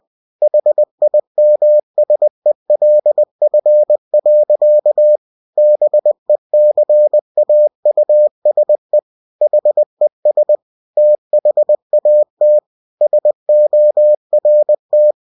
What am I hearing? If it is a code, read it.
Morse code, heard as a keyed tone, about 20 wpm, 600 Hz. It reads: HIMSELF. BECAUSE HES THAT SORT